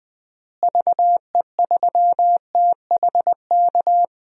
Morse code, heard as a keyed tone, about 20 wpm, 700 Hz. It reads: VE3THK
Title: DXCC-100 Call Signs - 6 Characters - Repeated 3x - 20wpm